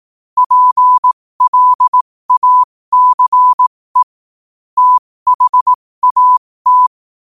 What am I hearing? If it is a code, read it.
Morse code, heard as a keyed tone, about 18 wpm, spaced as for 15 wpm, 1000 Hz. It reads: PLACE THAT
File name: qotdfiles / 15wpm